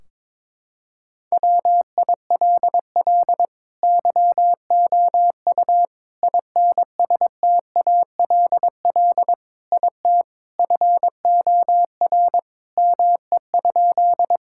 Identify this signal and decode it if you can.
Morse code, heard as a keyed tone, about 21 wpm, 700 Hz. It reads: WILL YOU INSTALL IT FOR ME?
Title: Sentences from Top 1600 Words 22wpm